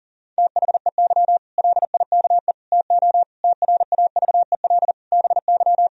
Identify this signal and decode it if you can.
Morse code, heard as a keyed tone, about 40 wpm, 700 Hz. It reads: THEY LIKE TO TRAVEL BY
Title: Sentences from Top 1000 Words - Review Part 3 - 40wpm